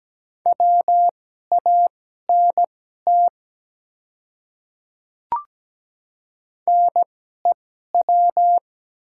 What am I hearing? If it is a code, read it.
Morse code, heard as a keyed tone, about 17 wpm, spaced as for 12 wpm, 700 Hz. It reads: WANT NEW
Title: 2x Character Spacing - Top 200 Words - 17wpm